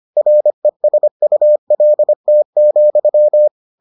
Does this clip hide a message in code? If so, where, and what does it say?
Yes: Morse code from 0.2 to 3.5 s, heard as a keyed tone, about 25 wpm, 600 Hz. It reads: RESULT,